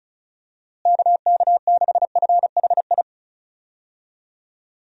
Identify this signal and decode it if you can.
Morse code, heard as a keyed tone, about 36 wpm, 700 Hz. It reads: KK6FHI
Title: USA Call Signs - 6 Characters - Repeated 2x - 35wpm